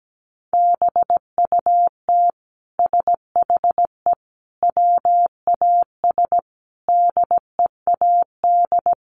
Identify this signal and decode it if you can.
Morse code, heard as a keyed tone, about 17 wpm, 700 Hz. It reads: BUT SHE WAS DEAD
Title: Sentences from Top 1500 Words 17wpm